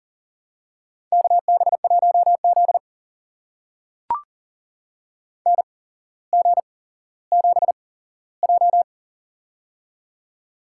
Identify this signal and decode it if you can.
Morse code, heard as a keyed tone, about 40 wpm, spaced as for 17 wpm, 700 Hz. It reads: KB1Z NG7J
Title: USA Call Signs - 8x Character Spacing - 40wpm